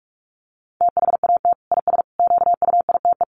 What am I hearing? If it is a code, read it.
Morse code, heard as a keyed tone, about 44 wpm, 700 Hz. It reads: THAT IS QUITE
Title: Sentences from Top 1100 Words 45wpm